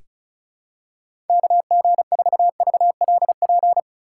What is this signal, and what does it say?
Morse code, heard as a keyed tone, about 34 wpm, 700 Hz. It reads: KG4VLP
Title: USA Call Signs - 6 Characters - 3x Character Spacing - 35wpm